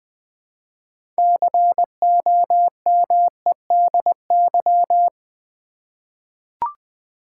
Morse code, heard as a keyed tone, about 20 wpm, 700 Hz. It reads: COMEDY
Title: 6 Letter Words - First 3 Letters Given - 20wpm